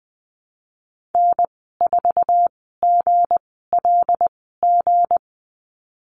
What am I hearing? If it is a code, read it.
Morse code, heard as a keyed tone, about 20 wpm, spaced as for 14 wpm, 700 Hz. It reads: N4GLG
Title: 161 Lesson 26g - Letter g - Call Signs 20wpm